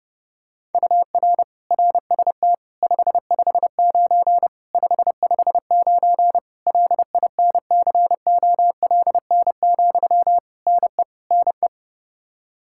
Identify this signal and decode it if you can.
Morse code, heard as a keyed tone, about 30 wpm, 700 Hz. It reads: UR RST 559 559 LINCOLN, NE NE
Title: Practice QSO - Exchange 1 - Beginning - 30wpm